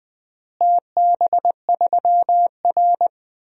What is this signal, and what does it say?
Morse code, heard as a keyed tone, about 21 wpm, 700 Hz. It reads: TB3R
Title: DXCC-100 Call Signs - 4 Characters - Repeated 3x - 20wpm